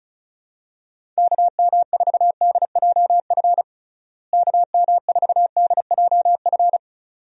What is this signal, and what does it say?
Morse code, heard as a keyed tone, about 36 wpm, 700 Hz. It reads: KM4DJF KM4DJF
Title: USA Call Signs - 6 Characters - Repeated 2x - 35wpm